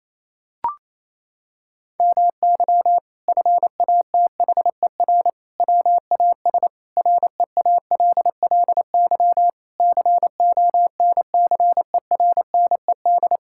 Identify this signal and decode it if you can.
Morse code, heard as a keyed tone, about 27 wpm, 700 Hz. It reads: MY FATHER WAS REALLY CONCERNED
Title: Sentences from Top 1000 Words - Review Part 6 - 28wpm